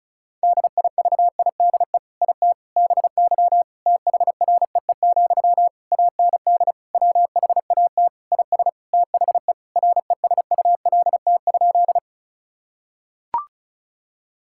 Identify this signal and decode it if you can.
Morse code, heard as a keyed tone, about 36 wpm, 700 Hz. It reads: DIVIDE IT BY THREE, AND WHAT IS THE RESULT?
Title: Sentences from Top 1600 Words 35wpm